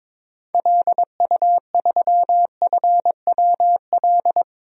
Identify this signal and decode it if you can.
Morse code, heard as a keyed tone, about 21 wpm, 700 Hz. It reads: LU3FWL
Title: DXCC-100 Call Signs - 6 Characters - 22wpm